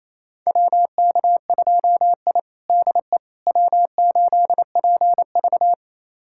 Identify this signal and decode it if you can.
Morse code, heard as a keyed tone, about 28 wpm, 700 Hz. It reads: WK2I DE W8PV
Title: USA Callsign DE Callsign - 4 Characters - 28wpm